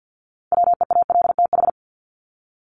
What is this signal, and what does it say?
Morse code, heard as a keyed tone, about 49 wpm, 700 Hz. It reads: WEALTH
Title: Top 1001-1250 Words 50wpm